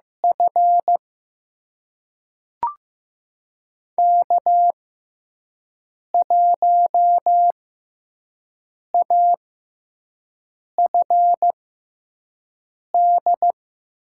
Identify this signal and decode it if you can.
Morse code, heard as a keyed tone, about 15 wpm, spaced as for 5 wpm, 700 Hz. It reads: F K1AFD
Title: USA Call Signs - 5 Characters - 6x Character Spacing - 15wpm